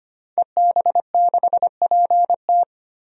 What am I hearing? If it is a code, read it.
Morse code, heard as a keyed tone, about 25 wpm, 700 Hz. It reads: EB6PT
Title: DXCC-100 Call Signs - 5 Characters - 25wpm